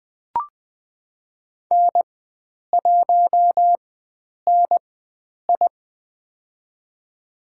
Morse code, heard as a keyed tone, about 21 wpm, spaced as for 9 wpm, 700 Hz. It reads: N1NI
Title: USA Call Signs - 4x Character Spacing - 20wpm